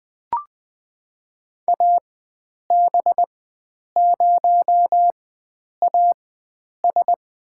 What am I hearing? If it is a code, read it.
Morse code, heard as a keyed tone, about 21 wpm, spaced as for 9 wpm, 700 Hz. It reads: AB0AS
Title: USA Call Signs - 5 Characters - 4x Character Spacing - 20wpm